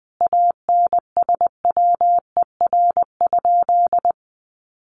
Morse code, heard as a keyed tone, about 20 wpm, 700 Hz. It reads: ANSWER?
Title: Sentences from Top 1300 Words 20wpm